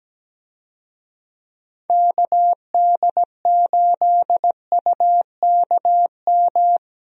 Morse code, heard as a keyed tone, about 17 wpm, 700 Hz. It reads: KD8UKM
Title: USA Call Signs - 6 Characters - Repeated 2x - 17wpm